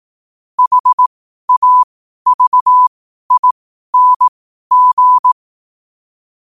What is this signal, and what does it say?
Morse code, heard as a keyed tone, about 18 wpm, spaced as for 12 wpm, 1000 Hz. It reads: HAVING
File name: qotdfiles / 12wpm